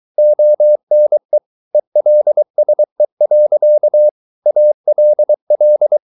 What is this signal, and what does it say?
Morse code, heard as a keyed tone, about 23 wpm, 600 Hz. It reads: ONE ELSE. ALL